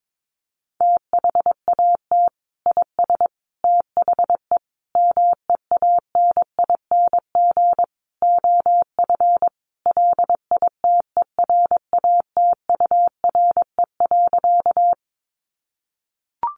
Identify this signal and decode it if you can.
Morse code, heard as a keyed tone, about 22 wpm, 700 Hz. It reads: THAT IS THE MEANING OF LITERATURE.
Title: Sentences from Top 900 Words 22wpm